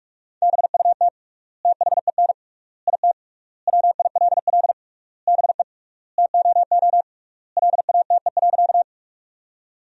Morse code, heard as a keyed tone, about 41 wpm, 700 Hz. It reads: BUT THEN IT WILL BE TOO LATE.
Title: 3x Word-Spacing - Sentences from Top 300 Words - 45wpm